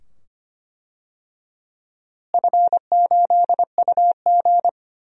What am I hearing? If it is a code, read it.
Morse code, heard as a keyed tone, about 25 wpm, 700 Hz. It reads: F8UG
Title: DXCC-100 Call Signs - 4 Characters - Repeated 2x - 25wpm